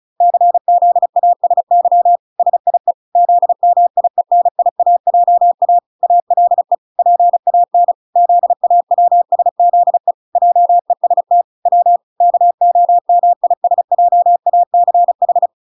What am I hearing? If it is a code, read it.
Morse code, heard as a keyed tone, about 34 wpm, 700 Hz. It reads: CZASY SIE ZMIENIAJA ALE PAN ZAWSZE JEST W KOMISJACH